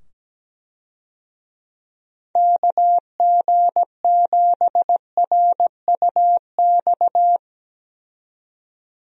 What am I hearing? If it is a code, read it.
Morse code, heard as a keyed tone, about 17 wpm, 700 Hz. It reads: KG7RUX
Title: USA Call Signs - 6 Characters - 17wpm